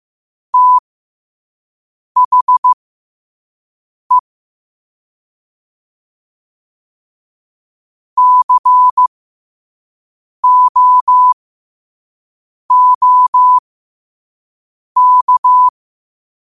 Morse code, heard as a keyed tone, about 15 wpm, spaced as for 5 wpm, 1000 Hz. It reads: THE COOK